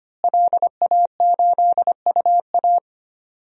Morse code, heard as a keyed tone, about 25 wpm, 700 Hz. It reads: LA8UA
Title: DXCC-100 Call Signs - 5 Characters - Repeated 3x - 25wpm